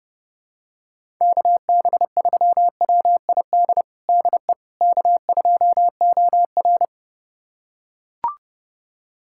Morse code, heard as a keyed tone, about 30 wpm, 700 Hz. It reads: KB3WID DE K2OR
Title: USA Callsign DE Callsign 30wpm